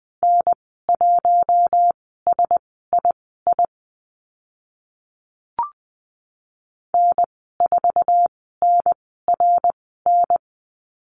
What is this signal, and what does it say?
Morse code, heard as a keyed tone, about 20 wpm, spaced as for 14 wpm, 700 Hz. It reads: N1SII N4NRN
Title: 062 Lesson 11g - Letter h - Call Signs 20wpm